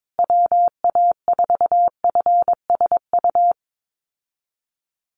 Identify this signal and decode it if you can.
Morse code, heard as a keyed tone, about 22 wpm, 700 Hz. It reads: WA4FSU
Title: USA Call Signs Repeated 2x 22wpm